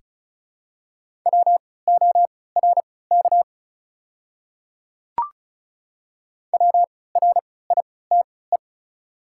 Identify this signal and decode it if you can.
Morse code, heard as a keyed tone, about 36 wpm, spaced as for 20 wpm, 700 Hz. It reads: WORK WRITE
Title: 3x Character Spacing - Top 200 Words - 35wpm